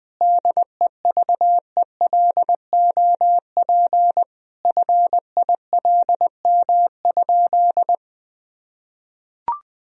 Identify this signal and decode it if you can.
Morse code, heard as a keyed tone, about 20 wpm, 700 Hz. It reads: DEVELOP FILM?
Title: Sentences from Top 1000 Words - Review Part 3 - 20wpm